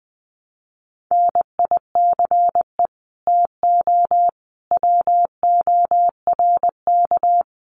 Morse code, heard as a keyed tone, about 19 wpm, 700 Hz. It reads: NICE TO WORK
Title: Common QSO Phrases 20wpm